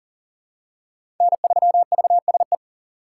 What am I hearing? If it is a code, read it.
Morse code, heard as a keyed tone, about 40 wpm, 700 Hz. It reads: N3VSE
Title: USA Call Signs - 5 Characters - 3x Character Spacing - 40wpm